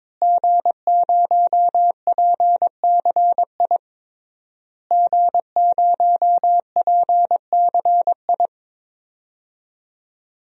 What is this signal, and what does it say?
Morse code, heard as a keyed tone, about 22 wpm, 700 Hz. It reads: G0PCI G0PCI
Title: DXCC-100 Call Signs - 5 Characters - Repeated 3x - 22wpm